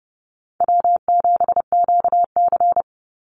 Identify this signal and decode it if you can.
Morse code, heard as a keyed tone, about 30 wpm, 700 Hz. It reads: W7QC
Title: Easy Call Signs - 30wpm